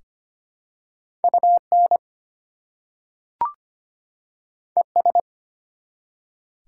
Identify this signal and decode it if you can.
Morse code, heard as a keyed tone, about 25 wpm, 700 Hz. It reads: UN ES
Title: Prefixes and Suffixes 25wpm